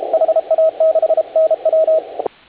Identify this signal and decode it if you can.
Morse code, heard as a keyed tone, about 34 wpm, 630 Hz, with radio noise. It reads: HA6NW
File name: HA6A CW